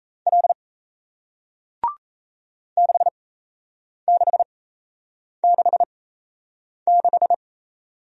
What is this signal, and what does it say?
Morse code, heard as a keyed tone, about 37 wpm, spaced as for 8 wpm, 700 Hz. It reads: L 6666